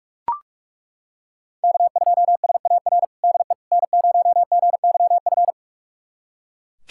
Morse code, heard as a keyed tone, about 42 wpm, 700 Hz. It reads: K2SAR DE N0GYF
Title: USA Callsign DE Callsign - 5 Characters - 45wpm